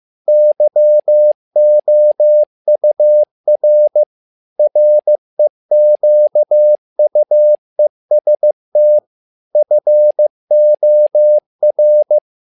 Morse code, heard as a keyed tone, about 15 wpm, 600 Hz. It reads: YOUR REQUEST FOR
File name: learning center / tower trouble / tower trouble 15wpm